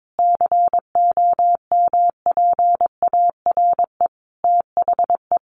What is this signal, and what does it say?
Morse code, heard as a keyed tone, about 22 wpm, 700 Hz. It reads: COMPARE THE
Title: Sentences from Top 1300 Words 22wpm